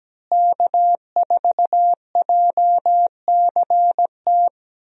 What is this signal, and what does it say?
Morse code, heard as a keyed tone, about 17 wpm, 700 Hz. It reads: K4JCT
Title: USA Call Signs - 5 Characters - Repeated 3x - 17wpm